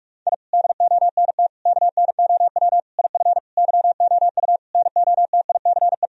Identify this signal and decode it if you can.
Morse code, heard as a keyed tone, about 44 wpm, 700 Hz. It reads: I DONT KNOW IF YOU NOTICE
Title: Sentences from Top 1000 Words - Review Part 2 - 45wpm